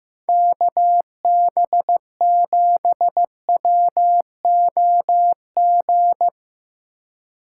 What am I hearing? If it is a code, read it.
Morse code, heard as a keyed tone, about 15 wpm, 700 Hz. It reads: KB7WOG
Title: Speed Racing - USA 2x3 Call Signs 15wpm